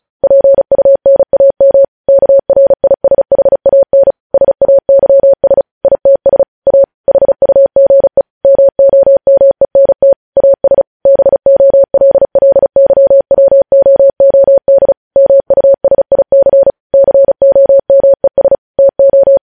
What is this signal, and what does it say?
Morse code, heard as a keyed tone, about 36 wpm, 550 Hz. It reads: PUNAM KRISHAN SAYS ITS A HUGE MOMENT AS BOLLYWOOD MUSIC COMES TO